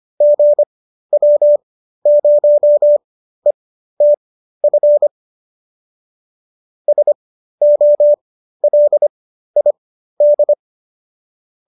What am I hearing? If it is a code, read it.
Morse code, heard as a keyed tone, about 25 wpm, spaced as for 13 wpm, 570 Hz. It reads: GW0ETF SOLID